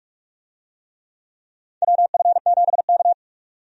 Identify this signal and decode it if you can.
Morse code, heard as a keyed tone, about 44 wpm, 700 Hz. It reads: WF7K
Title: USA Call Signs - 4 Characters - 45wpm